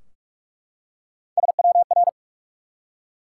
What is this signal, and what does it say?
Morse code, heard as a keyed tone, about 44 wpm, 700 Hz. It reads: SWR
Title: Top 180 QSO Elements 45wpm